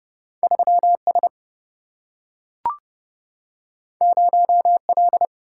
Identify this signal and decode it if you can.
Morse code, heard as a keyed tone, about 30 wpm, 700 Hz. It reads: 3 S 0 L
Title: Single-Letter Single-Number 30wpm